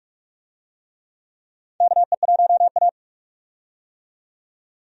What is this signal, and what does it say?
Morse code, heard as a keyed tone, about 46 wpm, 700 Hz. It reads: KE1A
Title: USA Call Signs - 4 Characters - 45wpm